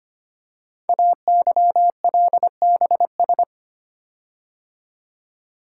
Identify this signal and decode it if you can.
Morse code, heard as a keyed tone, about 25 wpm, 700 Hz. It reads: AYLBS
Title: Code Groups - 5 Letter:Numbers - 25wpm